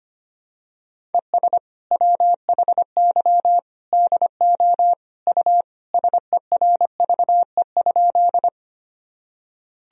Morse code, heard as a keyed tone, about 25 wpm, 700 Hz. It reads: ES WHY DO U SERVE?
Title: Sentences from Top 400 Words and Abbreviations 25wpm